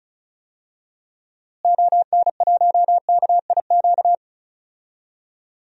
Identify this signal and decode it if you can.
Morse code, heard as a keyed tone, about 36 wpm, 700 Hz. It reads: ON1KIQ
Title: DXCC-100 Call Signs - 6 Characters - Repeated 2x - 35wpm